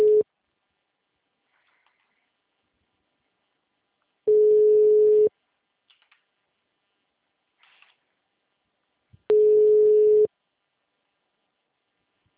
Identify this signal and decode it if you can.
Ringback tone (Europe, 425 Hz).
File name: out- -2031